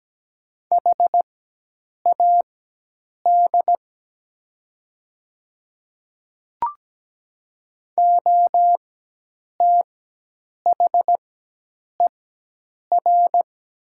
Morse code, heard as a keyed tone, about 17 wpm, spaced as for 8 wpm, 700 Hz. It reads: HAD OTHER
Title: 4x Character Spacing - Top 100 Words - 17wpm